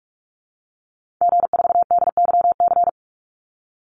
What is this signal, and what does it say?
Morse code, heard as a keyed tone, about 44 wpm, 700 Hz. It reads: G4DYC
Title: Call Signs Repeated 3x - 45wpm